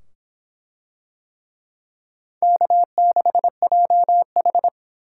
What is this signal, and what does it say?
Morse code, heard as a keyed tone, about 24 wpm, 700 Hz. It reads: K6JH